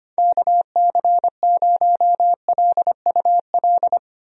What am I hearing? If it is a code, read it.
Morse code, heard as a keyed tone, about 25 wpm, 700 Hz. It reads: KC0LUL
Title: USA Call Signs - 6 Characters - 25wpm